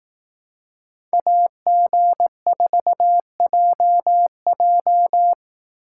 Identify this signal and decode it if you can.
Morse code, heard as a keyed tone, about 18 wpm, 700 Hz. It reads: AG4JJ
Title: Speed Racing - USA 2x2 Call Signs 15wpm